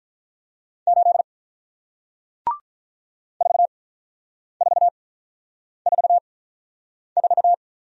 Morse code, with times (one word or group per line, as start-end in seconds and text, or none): 0.9-1.2 8
3.4-7.5 4444